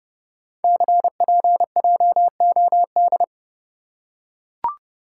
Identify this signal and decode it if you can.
Morse code, heard as a keyed tone, about 30 wpm, 700 Hz. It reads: CPJOD
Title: Code Groups - 5 Letters - 30wpm